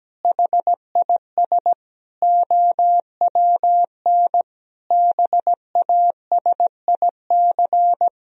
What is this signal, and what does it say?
Morse code, heard as a keyed tone, about 17 wpm, 700 Hz. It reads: HIS OWN BASIC
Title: Sentences from Top 1000 Words - Review Part 3 - 17wpm